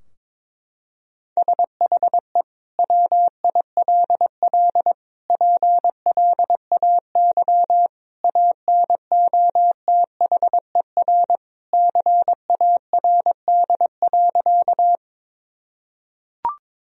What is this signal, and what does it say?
Morse code, heard as a keyed tone, about 22 wpm, 700 Hz. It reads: SHE WILL PLAY ANOTHER CARD.
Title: Sentences from Top 1000 Words - Review Part 4 - 22wpm